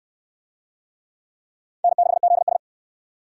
Morse code, heard as a keyed tone, about 67 wpm, 700 Hz. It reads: N6ZS